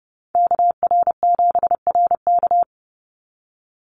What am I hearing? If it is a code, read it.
Morse code, heard as a keyed tone, about 30 wpm, 700 Hz. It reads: KR7RK
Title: NA SOTA Chaser Callsigns 30wpm